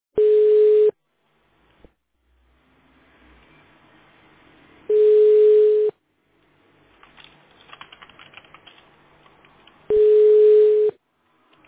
Ringback tone (Europe, 425 Hz).